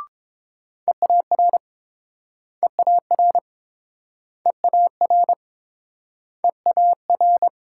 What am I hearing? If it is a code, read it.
Morse code, heard as a keyed tone, about 30 wpm, 700 Hz. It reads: EAR EAR EAR EAR